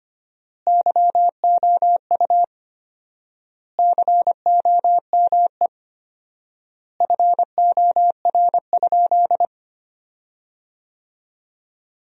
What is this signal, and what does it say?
Morse code, heard as a keyed tone, about 25 wpm, 700 Hz. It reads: YOU COME FOR?
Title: 4x Word-Spacing - Sentences from Top 100 Words - 25wpm